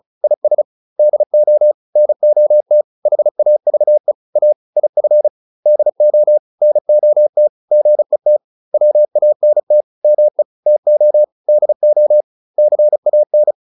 Morse code, heard as a keyed tone, about 34 wpm, 600 Hz. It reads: IS DO NOT HAVE A IF DO NOT GET WANT ME TO DO CAN